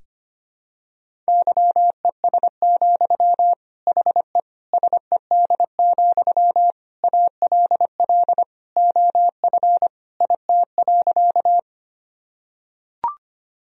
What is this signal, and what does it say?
Morse code, heard as a keyed tone, about 25 wpm, 700 Hz. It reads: YES, HE SED, ALL OF IT.
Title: Sentences from Top 500 Words and Abbreviations 25wpm